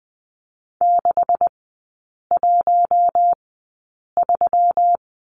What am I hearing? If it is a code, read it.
Morse code, heard as a keyed tone, about 20 wpm, spaced as for 8 wpm, 700 Hz. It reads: 613
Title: 133 Lesson 22e - Question Mark - Three Characters 20wpm